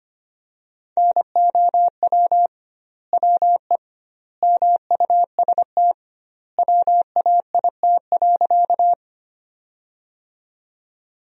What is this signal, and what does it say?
Morse code, heard as a keyed tone, about 25 wpm, 700 Hz. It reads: NOW WE MUST WAIT.
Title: 2x Word-Spacing - Sentences from Top 500 Words - 25wpm